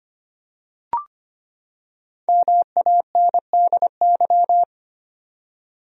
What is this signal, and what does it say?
Morse code, heard as a keyed tone, about 25 wpm, 700 Hz. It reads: MANDY
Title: USA Common Names - 25wpm